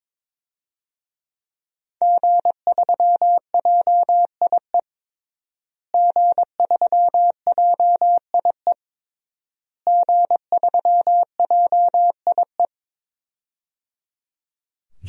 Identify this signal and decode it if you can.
Morse code, heard as a keyed tone, about 22 wpm, 700 Hz. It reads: G3JIE G3JIE G3JIE
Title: DXCC-100 Call Signs - 5 Characters - Repeated 3x - 22wpm